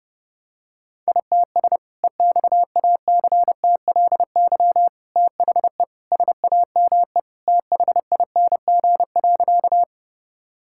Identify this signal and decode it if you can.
Morse code, heard as a keyed tone, about 32 wpm, 700 Hz. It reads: ITS EXACTLY THE SAME THING.
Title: Sentences from Top 1000 Words - Review Part 6 - 30wpm